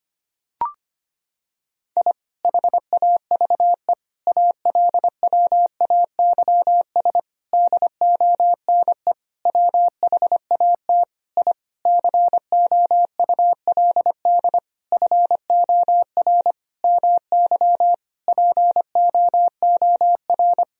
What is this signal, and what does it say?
Morse code, heard as a keyed tone, about 25 wpm, 700 Hz. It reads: I HAVE ALWAYS DONE WHAT I COULD FOR MY POOR
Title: Sentences from Top 1000 Words - Review Part 2 - 25wpm